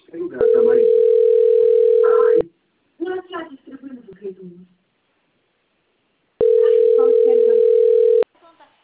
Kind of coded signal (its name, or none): telephone call progress tone